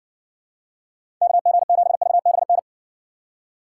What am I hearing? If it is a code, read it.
Morse code, heard as a keyed tone, about 60 wpm, 700 Hz. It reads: KG7VBN